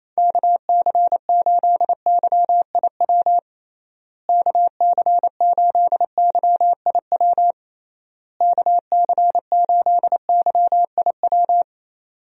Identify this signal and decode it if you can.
Morse code, heard as a keyed tone, about 28 wpm, 700 Hz. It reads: KC8YIW KC8YIW KC8YIW
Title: USA Call Signs - 6 Characters - Repeated 3x - 28wpm